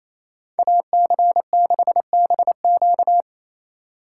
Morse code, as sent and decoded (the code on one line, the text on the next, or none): .- -.-. -.... -... --.-
AC6BQ